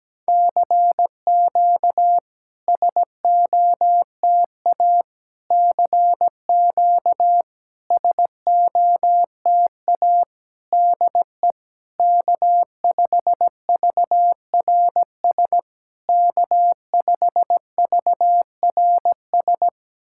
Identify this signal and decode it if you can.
Morse code, heard as a keyed tone, about 17 wpm, 700 Hz. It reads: CQ SOTA CQ SOTA DE K5VRS K5VRS
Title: SOTA QSO Practice 17wpm